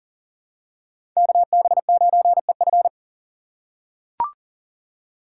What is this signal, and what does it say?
Morse code, heard as a keyed tone, about 40 wpm, 690 Hz. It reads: KB9EF
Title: USA Call Signs - 5 Characters - 6x Character Spacing - 40wpm